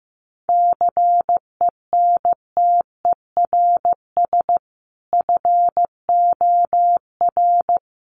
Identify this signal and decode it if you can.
Morse code, heard as a keyed tone, about 14 wpm, 700 Hz. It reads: CENTERS FOR
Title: Sets of 5 Words Encore2 15wpm